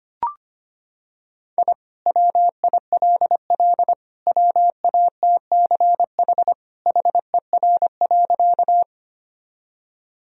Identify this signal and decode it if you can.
Morse code, heard as a keyed tone, about 25 wpm, 700 Hz. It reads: I WILL WATCH HER.
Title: Sentences from Top 1000 Words - Review Part 1 - 25wpm